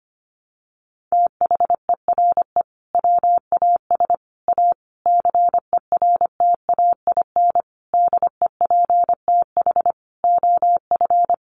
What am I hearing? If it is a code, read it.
Morse code, heard as a keyed tone, about 25 wpm, 700 Hz. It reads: THERE WAS A CERTAIN DEPTH OF
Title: Sentences from Top 1200 Words 25wpm